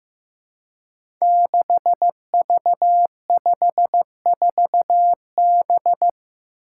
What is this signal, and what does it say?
Morse code, heard as a keyed tone, about 15 wpm, 700 Hz. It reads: 6V54B
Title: Five Challenging Letter-Number Combinations 15wpm